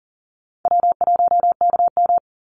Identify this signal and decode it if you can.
Morse code, heard as a keyed tone, about 40 wpm, 700 Hz. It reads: W1KM
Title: Call Signs Repeated 3x - 40wpm